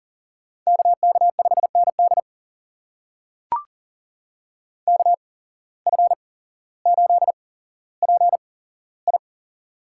Morse code, heard as a keyed tone, about 40 wpm, spaced as for 17 wpm, 700 Hz. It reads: KK5ND KF8PI
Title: USA Call Signs - 5 Characters - 8x Character Spacing - 40wpm